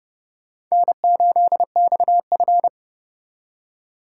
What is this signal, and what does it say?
Morse code, heard as a keyed tone, about 36 wpm, 700 Hz. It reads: N8XF
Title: Speed Racing - USA Call Signs 30wpm